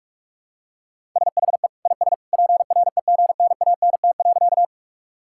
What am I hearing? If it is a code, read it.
Morse code, heard as a keyed tone, about 46 wpm, 700 Hz. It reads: SHE IS PREGNANT.
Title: Sentences from Top 1000 Words - Review Part 6 - 45wpm